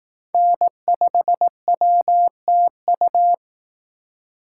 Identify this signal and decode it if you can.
Morse code, heard as a keyed tone, about 18 wpm, 700 Hz. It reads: N5WTU
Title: Speed Racing - USA 1x3 Call Signs 15wpm